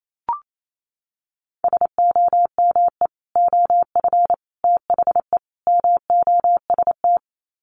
Morse code, heard as a keyed tone, about 28 wpm, 700 Hz. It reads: SOME OF THE MOST
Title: Sets of 4 Words 28wpm